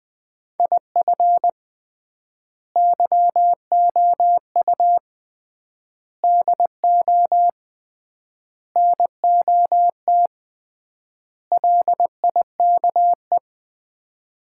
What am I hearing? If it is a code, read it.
Morse code, heard as a keyed tone, about 20 wpm, 690 Hz. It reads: IF YOU DO NOT LIKE
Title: 3x Word-Spacing - Sets of 5 Words - 20wpm